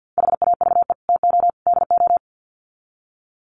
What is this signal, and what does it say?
Morse code, heard as a keyed tone, about 50 wpm, 700 Hz. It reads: HAVE TO DO
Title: Sets of 3 Words 50wpm